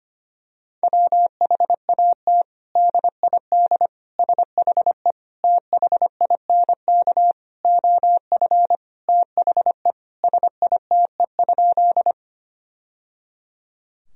Morse code, heard as a keyed tone, about 25 wpm, 700 Hz. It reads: WHAT DID SHE THINK OF THE SITE?
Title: Sentences from Top 1000 Words - Review Part 4 - 25wpm